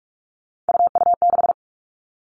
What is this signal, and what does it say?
Morse code, heard as a keyed tone, about 44 wpm, 700 Hz. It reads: UU6
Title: All Cut Numbers - 3-Digit 45wpm